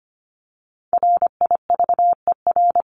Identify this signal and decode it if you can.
Morse code, heard as a keyed tone, about 25 wpm, 700 Hz. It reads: RIVER
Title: Top 500 Words 25wpm